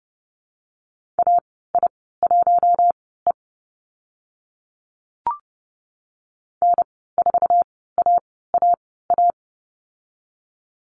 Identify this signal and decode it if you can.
Morse code, heard as a keyed tone, about 30 wpm, spaced as for 17 wpm, 700 Hz. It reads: AI1E N4AAA
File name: Lesson 14g - Number 2 - Call Signs 30wpm